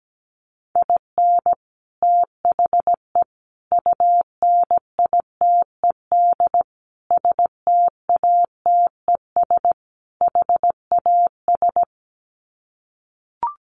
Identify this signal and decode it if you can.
Morse code, heard as a keyed tone, about 17 wpm, 700 Hz. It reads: IN THE UNITED STATES HAS
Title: Sets of 5 Words Encore2 17wpm